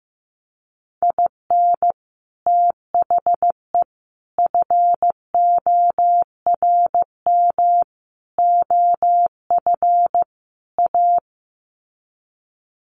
Morse code, heard as a keyed tone, about 14 wpm, 700 Hz. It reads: IN THE FORM OF A
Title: Sets of 5 Words 15wpm